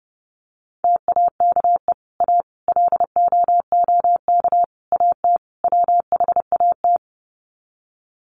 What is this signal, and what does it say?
Morse code, heard as a keyed tone, about 30 wpm, 700 Hz. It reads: TAKE A LOOK AT WHAT
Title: Sets of 5 Words Encore2 30wpm